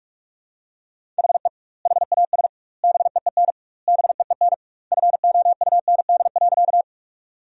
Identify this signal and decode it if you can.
Morse code, heard as a keyed tone, about 46 wpm, 700 Hz. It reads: HE HAS BEEN BEEN FOUND.
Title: 2x Word-Spacing - Sentences from Top 300 Words - 45wpm